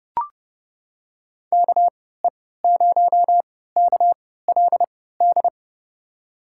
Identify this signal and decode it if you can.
Morse code, heard as a keyed tone, about 30 wpm, spaced as for 17 wpm, 700 Hz. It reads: KE0KLD
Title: USA Call Signs - 3x Character Spacing - 30wpm